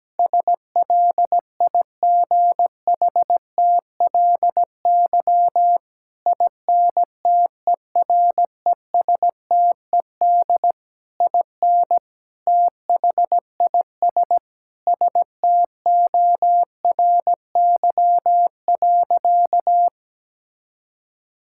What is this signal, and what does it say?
Morse code, heard as a keyed tone, about 17 wpm, 700 Hz. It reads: SLIGHTLY INTERESTED IN THIS STORY.
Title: Sentences from Top 1600 Words 17wpm